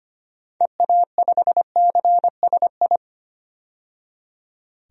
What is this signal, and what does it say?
Morse code, heard as a keyed tone, about 25 wpm, 700 Hz. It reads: EA5CSI
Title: DXCC-100 Call Signs - 6 Characters - 25wpm